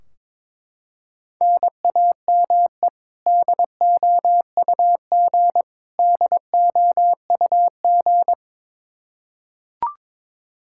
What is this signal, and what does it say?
Morse code, heard as a keyed tone, about 21 wpm, 700 Hz. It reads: NAME DOUG DOUG
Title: Practice QSO - Exchange 1 - Beginning - 22wpm